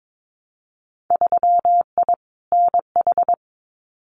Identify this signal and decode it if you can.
Morse code, heard as a keyed tone, about 22 wpm, 700 Hz. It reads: 3I NH
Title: Winter Field Day Exchanges Encore 22wpm